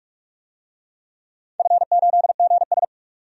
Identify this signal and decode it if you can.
Morse code, heard as a keyed tone, about 44 wpm, 700 Hz. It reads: F8GS
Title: DXCC-100 Call Signs - 4 Characters - 45wpm